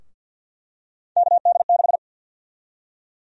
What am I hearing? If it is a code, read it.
Morse code, heard as a keyed tone, about 52 wpm, 700 Hz. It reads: KD6
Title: USA Call Sign Prefixes - Repeated 2x - 50wpm